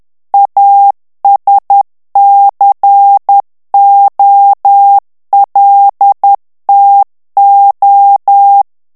Morse code, heard as a keyed tone, about 10 wpm, 800 Hz. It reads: ASCOLTO
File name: morse mp3 / ITALIEN texte clair 1 vitesse 10 mots minute